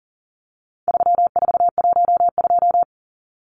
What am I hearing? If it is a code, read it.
Morse code, heard as a keyed tone, about 40 wpm, 700 Hz. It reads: 3412
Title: Numbers - 4-Digit 40wpm